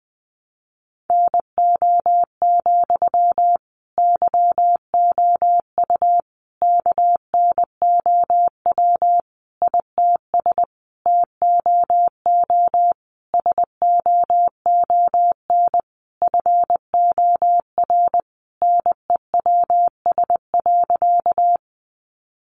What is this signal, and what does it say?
Morse code, heard as a keyed tone, about 20 wpm, 700 Hz. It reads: NO, YOU KNOW ITS TOO SOON FOR NEWS.
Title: Sentences from Top 800 Words 20wpm